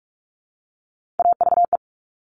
Morse code, heard as a keyed tone, about 46 wpm, 700 Hz. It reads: AVE
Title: Top Three Letter Combinations in Words 45wpm